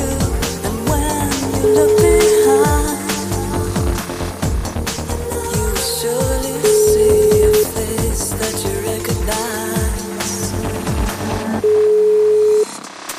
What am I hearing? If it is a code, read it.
Ringback tone (Europe, 425 Hz), with noise.